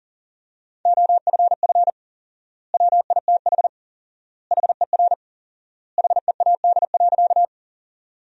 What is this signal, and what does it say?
Morse code, heard as a keyed tone, about 40 wpm, 700 Hz. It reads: OFF WITH HER HEAD.
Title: 4x Word-Spacing - Sentences from Top 300 Words - 40wpm